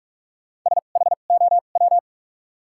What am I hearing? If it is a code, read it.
Morse code, heard as a keyed tone, about 43 wpm, spaced as for 31 wpm, 700 Hz. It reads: SHOW
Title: 2x Character Spacing - Top 500 Words - 45wpm